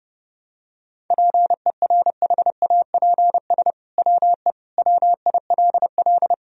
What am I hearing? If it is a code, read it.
Morse code, heard as a keyed tone, about 30 wpm, 700 Hz. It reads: PERHAPS WE WILL
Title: Sentences from Top 1000 Words - Review Part 2 - 30wpm